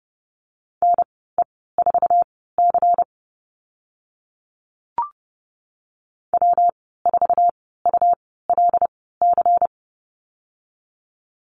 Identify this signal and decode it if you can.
Morse code, heard as a keyed tone, about 30 wpm, spaced as for 17 wpm, 700 Hz. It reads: NE4C W4ULC
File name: Lesson 19g - Letter w - Call Signs 30wpm